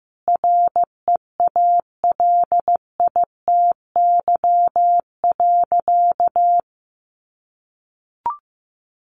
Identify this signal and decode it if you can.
Morse code, heard as a keyed tone, about 15 wpm, 700 Hz. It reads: REALITY.
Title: Sentences from Top 1200 Words 15wpm